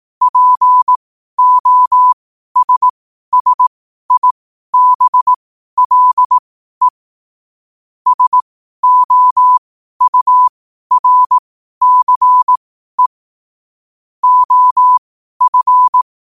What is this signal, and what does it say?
Morse code, heard as a keyed tone, about 18 wpm, spaced as for 12 wpm, 1000 Hz. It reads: POSSIBLE SOURCE OF
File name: qotdfiles / 12wpm